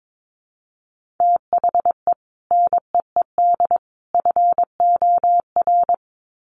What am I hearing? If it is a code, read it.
Morse code, heard as a keyed tone, about 22 wpm, 700 Hz. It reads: THE NEED FOR
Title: Sets of 3 Words 22wpm